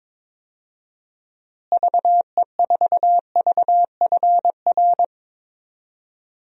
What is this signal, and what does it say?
Morse code, heard as a keyed tone, about 22 wpm, 700 Hz. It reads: VE4VFR
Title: DXCC-100 Call Signs - 6 Characters - Repeated 2x - 22wpm